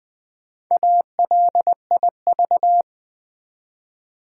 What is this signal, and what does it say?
Morse code, heard as a keyed tone, about 20 wpm, 700 Hz. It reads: ALIV
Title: Build A Word - 1 Letter At A Time - 5-Letter Words - 20wpm